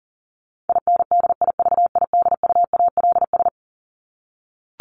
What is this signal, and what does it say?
Morse code, heard as a keyed tone, about 40 wpm, 700 Hz. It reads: INDIVIDUALS
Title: ES:S Suffix - Top 100 Words with Base Words 40wpm